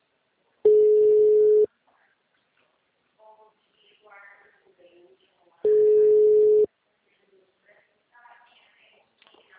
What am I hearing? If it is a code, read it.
Ringback tone (Europe, 425 Hz), with speech.